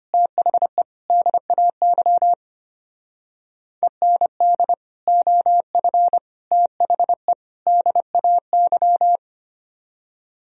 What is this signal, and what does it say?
Morse code, heard as a keyed tone, about 33 wpm, 700 Hz. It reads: THE DAY END OF THE DAY